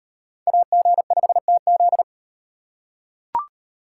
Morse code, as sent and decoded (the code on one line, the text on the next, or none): .- --. ..... - --..
AG5TZ